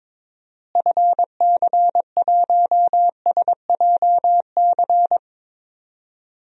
Morse code, heard as a keyed tone, about 22 wpm, 700 Hz. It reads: FC1SJC
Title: DXCC-100 Call Signs - 6 Characters - Repeated 3x - 22wpm